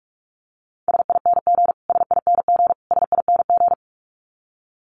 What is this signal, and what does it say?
Morse code, heard as a keyed tone, about 44 wpm, 700 Hz. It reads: SING SING SING
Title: Top 500 Words Repeated 45wpm